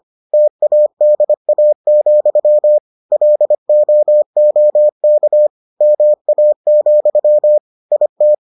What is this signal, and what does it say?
Morse code, heard as a keyed tone, about 25 wpm, 600 Hz. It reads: TADA, LOOK MA, IT